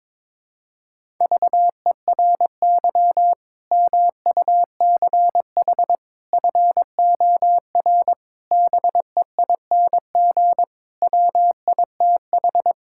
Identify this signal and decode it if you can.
Morse code, heard as a keyed tone, about 22 wpm, 700 Hz. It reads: VERY MUCH FOR BEING WITH
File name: practice / 22wpm / Build A Phrase - 5-Word Phrases - 22wpm